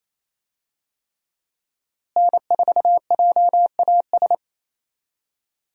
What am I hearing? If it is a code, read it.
Morse code, heard as a keyed tone, about 27 wpm, 700 Hz. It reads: N4JAS
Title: USA Call Signs - 5 Characters - Repeated 3x - 28wpm